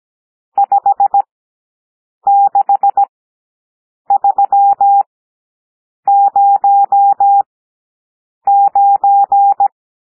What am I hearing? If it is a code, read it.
Morse code, heard as a keyed tone, about 17 wpm, spaced as for 7 wpm, 800 Hz. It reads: 56309
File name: SOUND NUMBERS 17 wpm